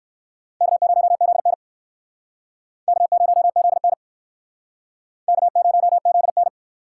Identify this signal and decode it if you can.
Morse code, heard as a keyed tone, about 59 wpm, 700 Hz. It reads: K0ZN K0ZN K0ZN